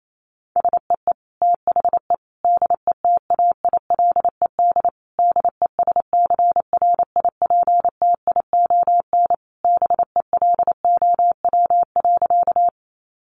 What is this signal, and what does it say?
Morse code, heard as a keyed tone, about 28 wpm, 700 Hz. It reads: SEE THE DETAILED DESCRIPTION BELOW.
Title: Sentences from Top 1500 Words 28wpm